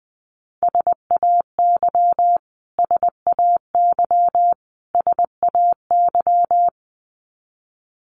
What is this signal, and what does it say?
Morse code, heard as a keyed tone, about 20 wpm, 700 Hz. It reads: SAY SAY SAY
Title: Top 500 Words Repeated 20wpm